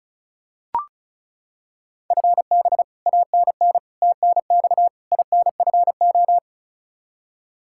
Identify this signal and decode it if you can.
Morse code, heard as a keyed tone, about 36 wpm, 700 Hz. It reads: FB ANN TNX INFO
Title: Practice QSO - Exchange 3 - More Info - 35wpm